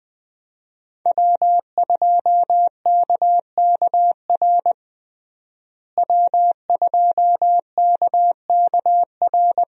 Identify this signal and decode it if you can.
Morse code, heard as a keyed tone, about 20 wpm, 700 Hz. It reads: W2KKR W2KKR
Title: USA Call Signs - 5 Characters - Repeated 2x - 20wpm